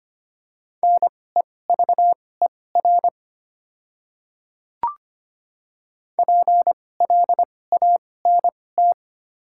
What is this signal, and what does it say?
Morse code, heard as a keyed tone, about 25 wpm, spaced as for 18 wpm, 700 Hz. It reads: NEVER PLANT
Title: 2x Character Spacing - Top 300 Words - 25wpm